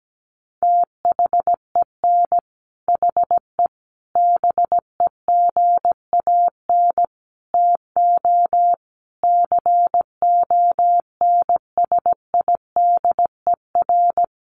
Morse code, heard as a keyed tone, about 17 wpm, 700 Hz. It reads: THEN HE BEGAN TO CONSIDER
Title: Sentences from Top 1200 Words 17wpm